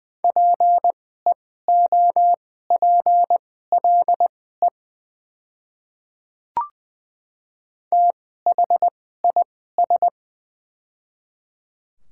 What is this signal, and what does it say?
Morse code, heard as a keyed tone, about 21 wpm, spaced as for 15 wpm, 700 Hz. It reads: PEOPLE THIS
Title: 2x Character Spacing - Top 100 Words - 20wpm